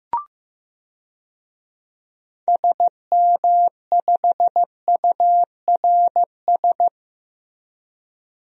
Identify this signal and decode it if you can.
Morse code, heard as a keyed tone, about 15 wpm, 700 Hz. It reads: SM5URS
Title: DXCC-100 Call Signs - 6 Characters - Repeated 3x - 15wpm